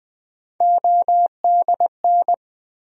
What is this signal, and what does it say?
Morse code, heard as a keyed tone, about 20 wpm, 700 Hz. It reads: ODN
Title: USA Call Sign Postfixes - 3 Letters 20wpm